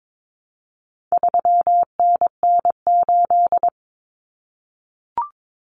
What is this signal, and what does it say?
Morse code, heard as a keyed tone, about 22 wpm, 700 Hz. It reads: 3NN8
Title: Common Cut Numbers - 4-Digit 22wpm